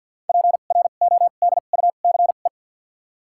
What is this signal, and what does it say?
Morse code, heard as a keyed tone, about 53 wpm, spaced as for 37 wpm, 700 Hz. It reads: PRODUCE
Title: 2x Character Spacing - Top 500 Words - 50wpm